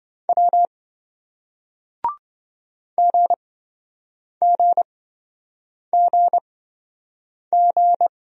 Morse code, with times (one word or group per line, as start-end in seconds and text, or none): 0.3-0.7 W
3.0-8.1 GGGG